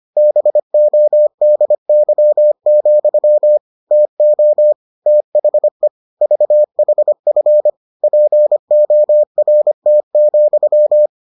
Morse code, heard as a keyed tone, about 25 wpm, 600 Hz. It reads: BODY, TO THE VHF PORT,